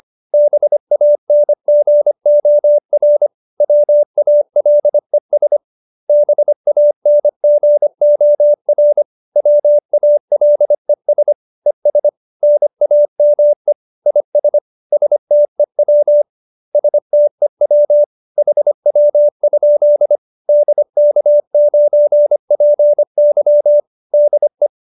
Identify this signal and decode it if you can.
Morse code, heard as a keyed tone, about 25 wpm, 590 Hz. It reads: BANGOR WALES BANGOR WALES ES NAME IS STEW STEW HW? DK9PY DE
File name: QSO318 25